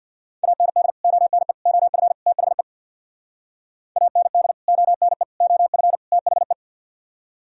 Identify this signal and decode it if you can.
Morse code, heard as a keyed tone, about 64 wpm, 700 Hz. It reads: AND ONE OF THE AND ONE OF THE